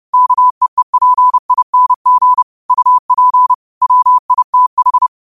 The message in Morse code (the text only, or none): KEEPING UP WITH